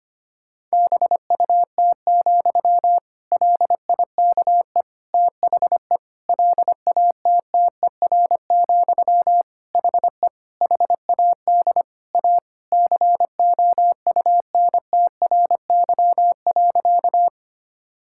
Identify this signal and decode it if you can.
Morse code, heard as a keyed tone, about 25 wpm, 700 Hz. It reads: BUT, LIKE THE LATTER, HE HAD A COUNTRY.
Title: Sentences from Top 1600 Words 25wpm